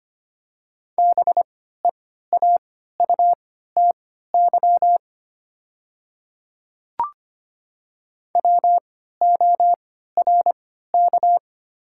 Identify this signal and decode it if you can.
Morse code, heard as a keyed tone, about 25 wpm, spaced as for 14 wpm, 690 Hz. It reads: BEAUTY WORK
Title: 3x Character Spacing - Top 500 Words - 25wpm